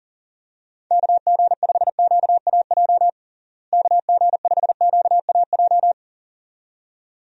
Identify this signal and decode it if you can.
Morse code, heard as a keyed tone, about 40 wpm, 700 Hz. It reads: KG5QAJ KG5QAJ
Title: USA Call Signs - 6 Characters - Repeated 2x - 40wpm